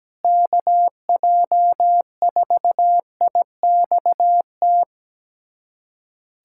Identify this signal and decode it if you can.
Morse code, heard as a keyed tone, about 17 wpm, 700 Hz. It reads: KJ4IXT
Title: USA Callsign DE Callsign - 6 Characters - 17wpm